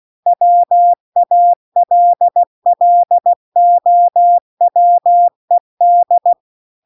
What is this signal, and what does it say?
Morse code, heard as a keyed tone, about 16 wpm, 700 Hz. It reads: WALLOWED
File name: CW-The-Big-Twoheaded-River-2-16WPM